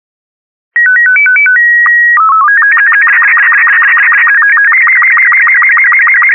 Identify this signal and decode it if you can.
SSTV (slow-scan television), Robot 36.